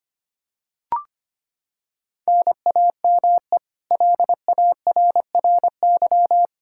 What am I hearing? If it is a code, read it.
Morse code, heard as a keyed tone, about 25 wpm, 700 Hz. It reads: NAME LARRY
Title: Practice QSO - Exchange 1 - Beginning - 25wpm